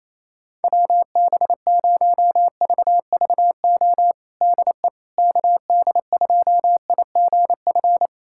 Morse code, heard as a keyed tone, about 28 wpm, 700 Hz. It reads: WB0VVO DE KD2IGF
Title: USA Callsign DE Callsign - 6 Characters - 28wpm